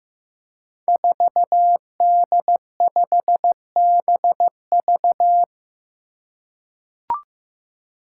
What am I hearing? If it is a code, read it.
Morse code, heard as a keyed tone, about 15 wpm, 700 Hz. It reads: 4D5BV
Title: Five Challenging Letter-Number Combinations 15wpm